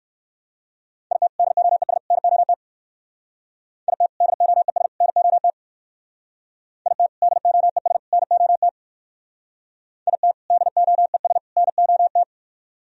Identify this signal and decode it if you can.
Morse code, heard as a keyed tone, about 54 wpm, 700 Hz. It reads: IT DOES NOT IT DOES NOT IT DOES NOT IT DOES NOT
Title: Speed-Racing - Sets of 3 Words 45wpm